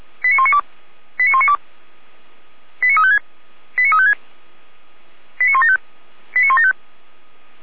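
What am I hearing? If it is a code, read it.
Five-tone selective call (ZVEI), heard as quick runs of five tones, tones of about 70 ms. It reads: 89192 89192 89356 89356 89175 89175
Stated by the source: ZVEI five-tone selective call received on SDR, from feuerwehr-orlamuende fire brigade server.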